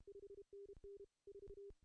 Morse code, heard as a keyed tone, about 31 wpm, 410 Hz. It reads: 5NN 4
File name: CW / cqww-cw-k5tr